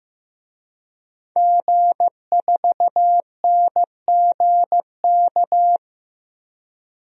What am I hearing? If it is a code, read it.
Morse code, heard as a keyed tone, about 15 wpm, 700 Hz. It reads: G4NGK
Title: DXCC-100 Call Signs - 5 Characters - Repeated 3x - 15wpm